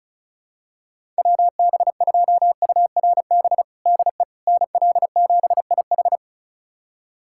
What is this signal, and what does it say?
Morse code, heard as a keyed tone, about 36 wpm, 700 Hz. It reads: WB2URB DE NL7IH
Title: USA Callsign DE Callsign 35wpm